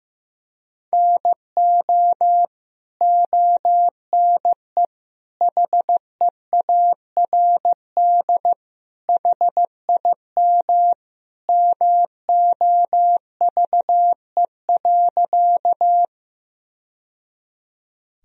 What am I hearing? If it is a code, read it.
Morse code, heard as a keyed tone, about 16 wpm, 700 Hz. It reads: NO ONE HEARD HIM MOVE.
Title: Sentences from Top 1000 Words - Review Part 1 - 15wpm